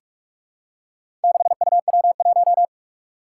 Morse code, heard as a keyed tone, about 44 wpm, 700 Hz. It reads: 6UW1
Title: Four Challenging Letter-Number Combinations 45wpm